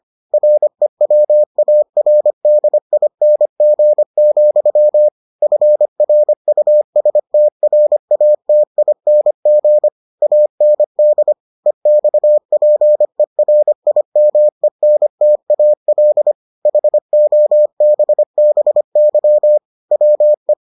Morse code, heard as a keyed tone, about 25 wpm, 600 Hz. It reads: REWARDING, FRUSTRATING AND EXPERIMENTAL HOBBY WE